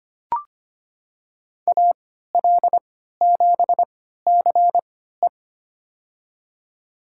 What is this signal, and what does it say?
Morse code, heard as a keyed tone, about 25 wpm, spaced as for 14 wpm, 700 Hz. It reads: AL7CE
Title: USA Call Signs - 5 Characters - 3x Character Spacing - 25wpm